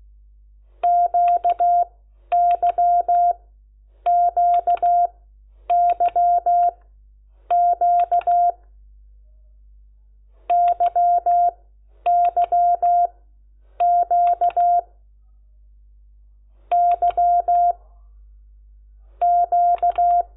Morse code, heard as a keyed tone, about 16 wpm, spaced as for 9 wpm, 700 Hz. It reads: QYQYQ YYQ Y Q
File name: Lecciones / CLASE Nº10